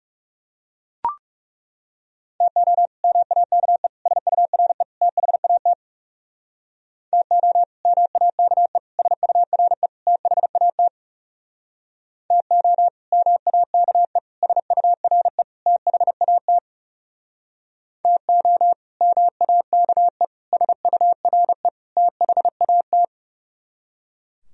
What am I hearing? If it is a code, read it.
Morse code, heard as a keyed tone, about 37 wpm, 700 Hz. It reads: TO MAKE SURE THAT TO MAKE SURE THAT TO MAKE SURE THAT TO MAKE SURE THAT
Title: Speed-Racing - Sets of 4 Words 30wpm